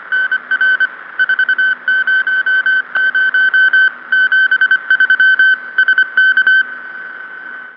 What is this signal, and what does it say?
Morse code, heard as a keyed tone, about 24 wpm, 1550 Hz. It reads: NR 400 73 SK